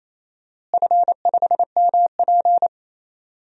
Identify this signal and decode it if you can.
Morse code, heard as a keyed tone, about 28 wpm, 700 Hz. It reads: F5MP
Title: DXCC-100 Call Signs - 4 Characters - Repeated 2x - 28wpm